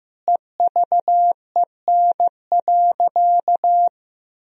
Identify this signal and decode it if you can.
Morse code, heard as a keyed tone, about 15 wpm, 700 Hz. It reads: EVEN.
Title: Sentences from Top 1000 Words - Review Part 1 - 15wpm